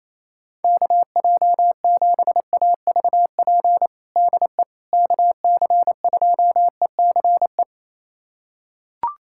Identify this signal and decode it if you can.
Morse code, heard as a keyed tone, about 28 wpm, 700 Hz. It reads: KJ7AVP DE KC2ECE
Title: USA Callsign DE Callsign 28wpm